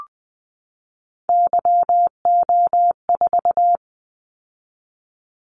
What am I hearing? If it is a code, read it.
Morse code, heard as a keyed tone, about 19 wpm, 700 Hz. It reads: YO4
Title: Callsign Prefixes for Top 25 DXCC - 20wpm